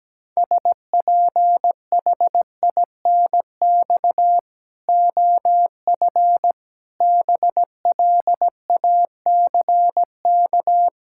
Morse code, heard as a keyed tone, about 17 wpm, 700 Hz. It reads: SPHINX OF BLACK
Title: Pangram Challenge v2 - 17wpm